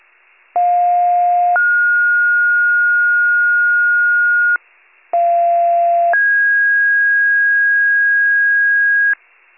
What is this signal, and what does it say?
Two-tone page, heard as a one-second tone, then a three-second tone, about 710 then 1430 Hz, about 690 then 1650 Hz, with noise. It is heard from 0.5 to 9.1 s.